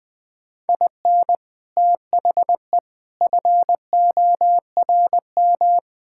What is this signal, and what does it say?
Morse code, heard as a keyed tone, about 20 wpm, 690 Hz. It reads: IN THE FORM
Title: Build A Phrase - 5-Word Phrases - 20wpm